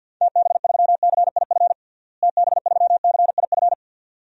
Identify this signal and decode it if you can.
Morse code, heard as a keyed tone, about 53 wpm, 700 Hz. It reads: TB3CIF TB3CIF
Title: DXCC-100 Call Signs - 6 Characters - Repeated 2x - 50wpm